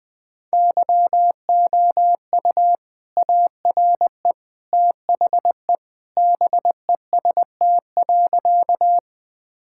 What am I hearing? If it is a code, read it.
Morse code, heard as a keyed tone, about 19 wpm, 700 Hz. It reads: YOU ARE THE BEST.
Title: Sentences with 2-Word Phrases - 20wpm